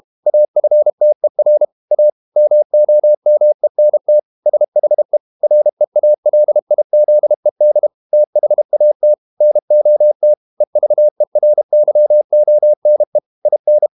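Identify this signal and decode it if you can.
Morse code, heard as a keyed tone, about 32 wpm, 600 Hz. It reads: AFTER A MOMENT SHE REALIZED THAT NOT EVERYONE IN
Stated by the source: Morse code practice at 32 wpm from CW med Gnister, reading Ken Follett's novel Never in English.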